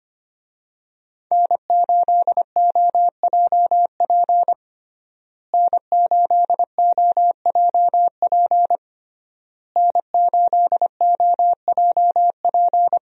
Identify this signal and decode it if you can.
Morse code, heard as a keyed tone, about 25 wpm, 700 Hz. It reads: N8OJP N8OJP N8OJP
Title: USA Call Signs - 5 Characters - Repeated 3x - 25wpm